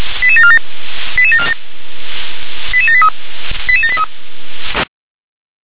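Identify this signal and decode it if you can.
Five-tone selective call (ZVEI), heard as quick runs of five tones, tones of about 70 ms. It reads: 88747 88747 88783 88783
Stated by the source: ZVEI five-tone call, code 88154, received by SDR.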